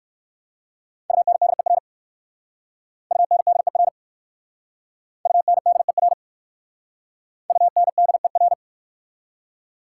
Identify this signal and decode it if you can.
Morse code, heard as a keyed tone, about 55 wpm, 700 Hz. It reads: UNDER UNDER UNDER UNDER
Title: Speed-Racing Prefixes and Suffixes 45wpm